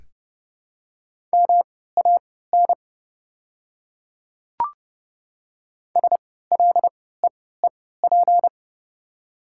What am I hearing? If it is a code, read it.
Morse code, heard as a keyed tone, about 30 wpm, spaced as for 17 wpm, 690 Hz. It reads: MAN SLEEP
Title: 3x Character Spacing - Top 500 Words - 30wpm